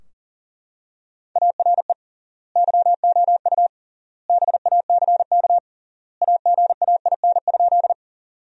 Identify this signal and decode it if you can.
Morse code, heard as a keyed tone, about 40 wpm, 700 Hz. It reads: ARE YOU BACK AGAIN?
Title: 3x Word-Spacing - Sentences from Top 200 Words - 40wpm